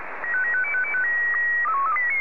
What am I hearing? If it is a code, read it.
SSTV (slow-scan television), Scottie 1.